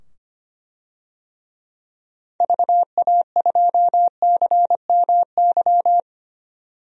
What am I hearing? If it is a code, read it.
Morse code, heard as a keyed tone, about 25 wpm, 700 Hz. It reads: VA2CMY
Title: DXCC-100 Call Signs - 6 Characters - 25wpm